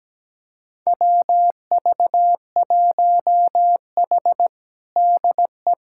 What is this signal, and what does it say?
Morse code, heard as a keyed tone, about 17 wpm, 700 Hz. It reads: WV1H DE
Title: USA Callsign DE Callsign - 4 Characters - 17wpm